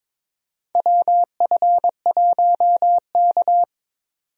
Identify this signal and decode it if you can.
Morse code, heard as a keyed tone, about 22 wpm, 700 Hz. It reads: WF1K
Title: USA Call Signs - 4 Characters - 22wpm